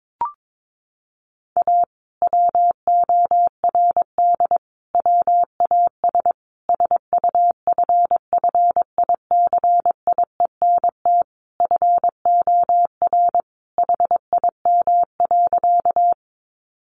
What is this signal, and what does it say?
Morse code, heard as a keyed tone, about 22 wpm, 700 Hz. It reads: A WORD WAS SUFFICIENT FOR HIM.
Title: Sentences from Top 1400 Words 22wpm